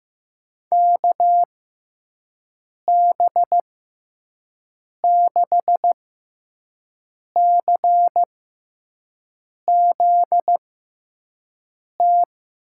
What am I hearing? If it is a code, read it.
Morse code, heard as a keyed tone, about 16 wpm, spaced as for 5 wpm, 700 Hz. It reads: KB6CZT
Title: USA Call Signs - 6 Characters - 6x Character Spacing - 15wpm